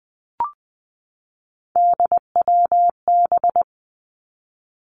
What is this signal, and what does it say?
Morse code, heard as a keyed tone, about 20 wpm, 700 Hz. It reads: DWB
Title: Three Challenging Letter-Number Combinations 20wpm